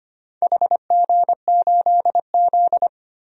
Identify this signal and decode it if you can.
Morse code, heard as a keyed tone, about 25 wpm, 690 Hz. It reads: HG8Z
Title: DXCC-100 Call Signs - 4 Characters - 25wpm